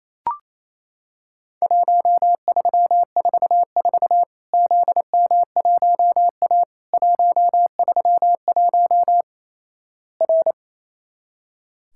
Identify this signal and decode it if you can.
Morse code, heard as a keyed tone, about 28 wpm, 700 Hz. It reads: 1344 ZM1A 131 R
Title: Worked All Europe DX Contest - QTC Exchange - 28wpm